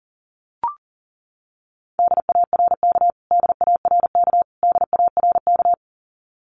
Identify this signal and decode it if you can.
Morse code, heard as a keyed tone, about 40 wpm, 700 Hz. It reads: DARK DARK DARK
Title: Top 500 Words Repeated 40wpm